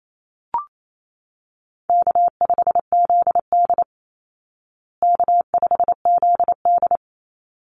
Morse code, heard as a keyed tone, about 28 wpm, 700 Hz. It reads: K5ZD K5ZD
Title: Easy Call Signs Repeated 2x - 28wpm